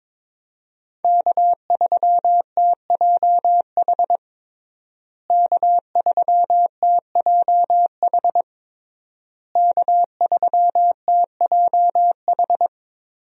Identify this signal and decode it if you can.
Morse code, heard as a keyed tone, about 22 wpm, 700 Hz. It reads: K3TJH K3TJH K3TJH
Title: USA Call Signs - 5 Characters - Repeated 3x - 22wpm